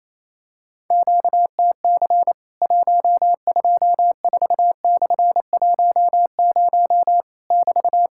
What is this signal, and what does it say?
Morse code, heard as a keyed tone, about 27 wpm, 700 Hz. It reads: QTC 124/10 <BT>
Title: Worked All Europe DX Contest - QTC Exchange - 28wpm